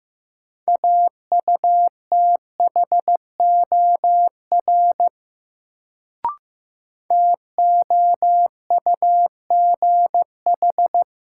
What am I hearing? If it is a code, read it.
Morse code, heard as a keyed tone, about 15 wpm, 700 Hz. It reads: AUTHOR TOUGH
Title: Sentences from Top 1200 Words - New Word Drill - 15wpm